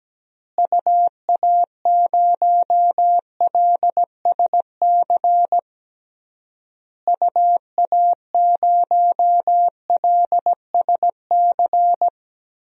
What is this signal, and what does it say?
Morse code, heard as a keyed tone, about 17 wpm, 700 Hz. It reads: UA0LSC UA0LSC
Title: DXCC-100 Call Signs - 6 Characters - Repeated 3x - 17wpm